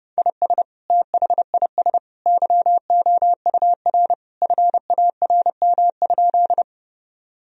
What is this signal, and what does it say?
Morse code, heard as a keyed tone, about 30 wpm, 700 Hz. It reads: IS THIS YOUR FARM?
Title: Sentences from Top 1000 Words - Review Part 2 - 30wpm